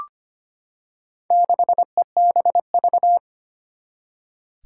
Morse code, heard as a keyed tone, about 25 wpm, 700 Hz. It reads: 6EBV
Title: Four Challenging Letter-Number Combinations 25wpm